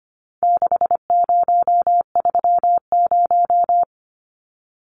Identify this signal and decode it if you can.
Morse code, heard as a keyed tone, about 25 wpm, 700 Hz. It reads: 6030
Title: Numbers - 4-Digit 25wpm